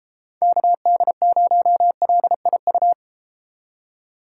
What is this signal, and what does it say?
Morse code, heard as a keyed tone, about 32 wpm, 700 Hz. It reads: KD0LIU